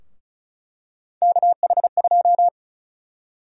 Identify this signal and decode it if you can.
Morse code, heard as a keyed tone, about 34 wpm, 700 Hz. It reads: KH2
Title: USA Call Sign Prefixes - Repeated 2x - 35wpm